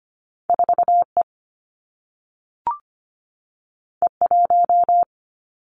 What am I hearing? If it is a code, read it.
Morse code, heard as a keyed tone, about 25 wpm, 700 Hz. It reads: 4 E E 1
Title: Two Challenging Letter-Number Combinations 25wpm